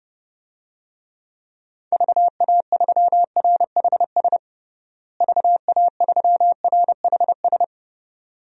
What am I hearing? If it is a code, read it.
Morse code, heard as a keyed tone, about 30 wpm, 700 Hz. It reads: VA3RHS VA3RHS
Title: DXCC-100 Call Signs - 6 Characters - Repeated 2x - 30wpm